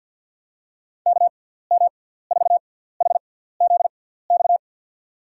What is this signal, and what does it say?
Morse code, heard as a keyed tone, about 51 wpm, spaced as for 17 wpm, 700 Hz. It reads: KM4HZX
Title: USA Call Signs - 6 Characters - 6x Character Spacing - 50wpm